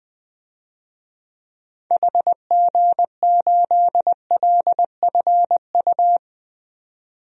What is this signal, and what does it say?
Morse code, heard as a keyed tone, about 21 wpm, 700 Hz. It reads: HG8LFU
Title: DXCC-100 Call Signs - 6 Characters - 20wpm